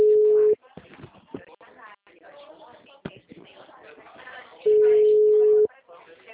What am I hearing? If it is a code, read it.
Ringback tone (Europe, 425 Hz), with speech.